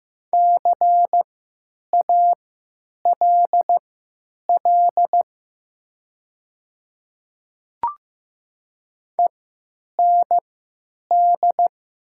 Morse code, heard as a keyed tone, about 15 wpm, spaced as for 9 wpm, 700 Hz. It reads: CALL END
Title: 3x Character Spacing - Top 200 Words - 15wpm